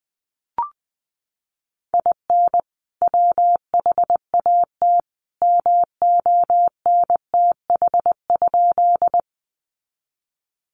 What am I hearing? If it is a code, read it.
Morse code, heard as a keyed tone, about 20 wpm, 700 Hz. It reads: IN WHAT MONTH?
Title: Sentences from Top 800 Words 20wpm